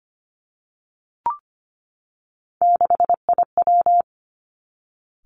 Morse code, heard as a keyed tone, about 25 wpm, 700 Hz. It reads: 6IW
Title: Three Challenging Letter-Number Combinations 25wpm